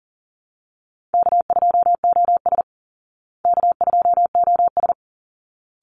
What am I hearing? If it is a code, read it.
Morse code, heard as a keyed tone, about 40 wpm, 700 Hz. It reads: K2OS K2OS
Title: Easy Call Signs Repeated 2x - 40wpm